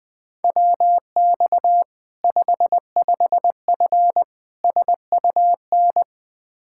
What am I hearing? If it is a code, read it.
Morse code, heard as a keyed tone, about 21 wpm, 700 Hz. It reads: WX 55F SUN
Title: Practice QSO - Exchange 2 - Info - 20wpm